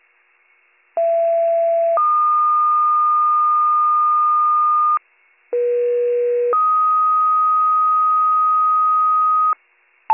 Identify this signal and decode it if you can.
Two-tone page, heard as a one-second tone, then a three-second tone, about 670 then 1220 Hz, about 480 then 1250 Hz. It is heard from 1.0 to 9.5 s.